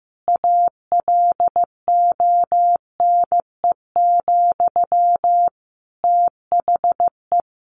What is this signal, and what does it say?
Morse code, heard as a keyed tone, about 14 wpm, 700 Hz. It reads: ALONE, THE
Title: Sentences from Top 1100 Words 15wpm